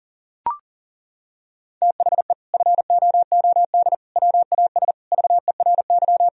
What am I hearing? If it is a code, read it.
Morse code, heard as a keyed tone, about 40 wpm, 700 Hz. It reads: THE FOOD WAS VERY
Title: Sentences from Top 1000 Words - Review Part 1 - 40wpm